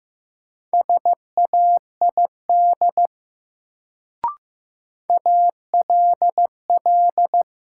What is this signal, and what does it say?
Morse code, heard as a keyed tone, about 15 wpm, 700 Hz. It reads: SAID ALL
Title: Sentences from Top 100 Words - New Word Drill - 15wpm